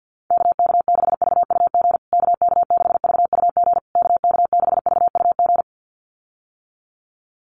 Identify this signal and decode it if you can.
Morse code, heard as a keyed tone, about 48 wpm, 700 Hz. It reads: KK6VUG KK6VUG KK6VUG
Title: USA Call Signs Repeated 3x 50wpm